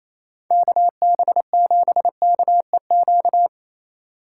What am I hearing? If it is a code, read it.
Morse code, heard as a keyed tone, about 36 wpm, 700 Hz. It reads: KB7KEQ